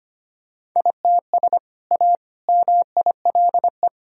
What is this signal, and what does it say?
Morse code, heard as a keyed tone, about 25 wpm, 700 Hz. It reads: ITS A MILE
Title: Sentences from Top 1000 Words - Review Part 1 - 25wpm